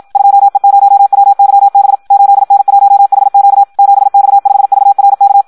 Morse code, heard as a keyed tone, about 56 wpm, 800 Hz. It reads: 2E0WYB 8N1HQ 7X5VRK
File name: Audio practica CW / qrq-tx-54 2